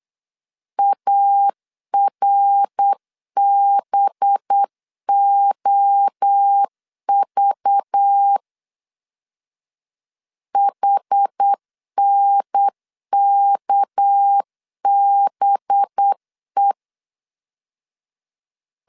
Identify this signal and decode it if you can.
Morse code, heard as a keyed tone, about 8 wpm, 800 Hz. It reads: ARBOV HNKBE
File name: LEZIONI / 6 PKX0 / 6 PK 8wpm 3003